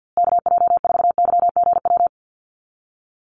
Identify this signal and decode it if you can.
Morse code, heard as a keyed tone, about 50 wpm, 700 Hz. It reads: KJ4YGW
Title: USA Call Signs Repeated 3x 50wpm